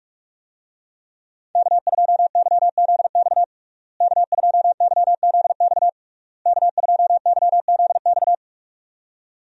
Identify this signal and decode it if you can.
Morse code, heard as a keyed tone, about 46 wpm, 700 Hz. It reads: K2YZX K2YZX K2YZX
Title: USA Call Signs - 5 Characters - Repeated 3x - 45wpm